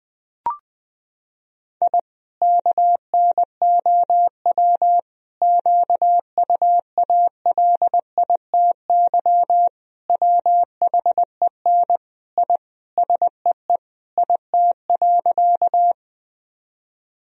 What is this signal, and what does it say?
Morse code, heard as a keyed tone, about 20 wpm, 700 Hz. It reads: I KNOW QUALITY WHEN I SEE IT.
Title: Sentences from Top 1000 Words - Review Part 4 - 20wpm